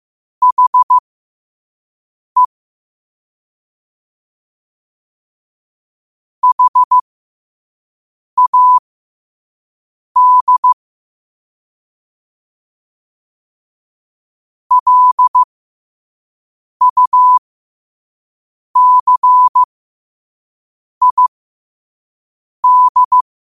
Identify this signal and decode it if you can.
Morse code, heard as a keyed tone, about 15 wpm, spaced as for 5 wpm, 1000 Hz. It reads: HE HAD LUCID
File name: qotdfiles / 5wpm